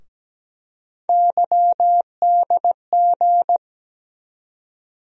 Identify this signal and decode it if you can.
Morse code, heard as a keyed tone, about 17 wpm, 700 Hz. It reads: YDG
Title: USA Call Sign Postfixes - 3 Letters 17wpm